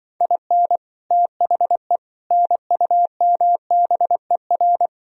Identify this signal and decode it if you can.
Morse code, heard as a keyed tone, about 23 wpm, 700 Hz. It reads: IN THE NUMBER
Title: Speed-Racing - Sets of 4 Words 20wpm